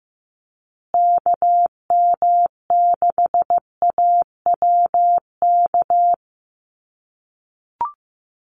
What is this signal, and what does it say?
Morse code, heard as a keyed tone, about 15 wpm, 700 Hz. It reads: KM6AWK
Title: USA Call Signs Repeated 2x 15wpm